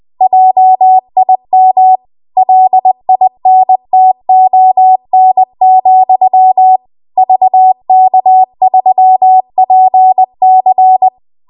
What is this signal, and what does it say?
Morse code, heard as a keyed tone, about 20 wpm, 750 Hz. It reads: JIM LINTON, VK3PC
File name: Archive / 20 WPM / 20WPM